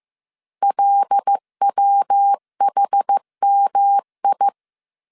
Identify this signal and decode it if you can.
Morse code, heard as a keyed tone, about 15 wpm, 800 Hz. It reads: LWHMI
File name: CW / DET / 1 9 1500C S A 14wpm 3222